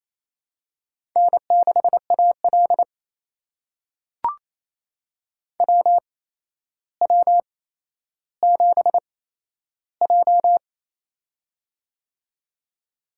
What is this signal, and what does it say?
Morse code, heard as a keyed tone, about 28 wpm, spaced as for 12 wpm, 700 Hz. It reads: N6AL WW7J
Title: USA Call Signs - 4 Characters - 8x Character Spacing - 28wpm